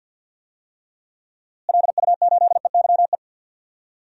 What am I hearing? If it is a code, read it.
Morse code, heard as a keyed tone, about 53 wpm, 700 Hz. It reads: LU8EYE
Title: DXCC-100 Call Signs - 6 Characters - Repeated 2x - 50wpm